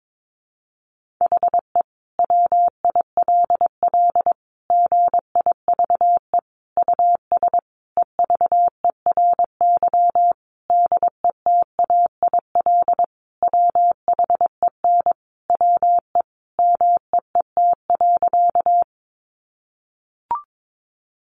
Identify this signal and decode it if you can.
Morse code, heard as a keyed tone, about 22 wpm, 700 Hz. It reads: HE WILL GIVE US EVERY DETAIL WHEN WE MEET.
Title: Sentences from Top 1500 Words 22wpm